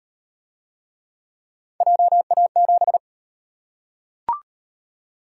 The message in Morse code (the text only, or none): JA7